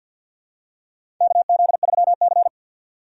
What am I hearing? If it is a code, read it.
Morse code, heard as a keyed tone, about 52 wpm, 700 Hz. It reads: KZ3C
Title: USA Call Signs - 2x Character Spacing - 50wpm